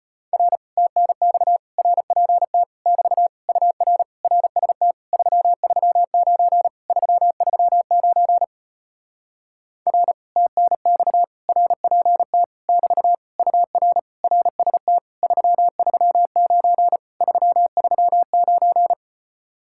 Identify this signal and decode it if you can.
Morse code, heard as a keyed tone, about 33 wpm, 700 Hz. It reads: R TNX RPT <BT> UR RST 339 339 R TNX RPT <BT> UR RST 339 339